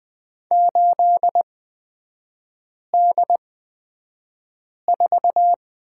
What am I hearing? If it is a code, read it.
Morse code, heard as a keyed tone, about 20 wpm, spaced as for 5 wpm, 700 Hz. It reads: 8D4
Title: 224 Lesson 36e - BT - Three Characters 20wpm